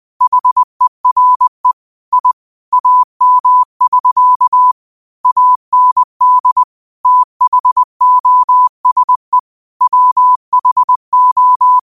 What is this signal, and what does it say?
Morse code, heard as a keyed tone, about 20 wpm, 1000 Hz. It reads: HERE I AM <SK> AND THOSE WHO
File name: qotdfiles / 20wpm